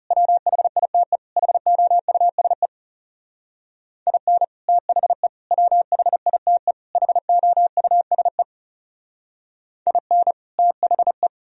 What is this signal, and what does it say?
Morse code, heard as a keyed tone, about 36 wpm, 700 Hz. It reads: WHITE HOUSE IN THE WHITE HOUSE IN THE
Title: Speed-Racing - Sets of 4 Words 30wpm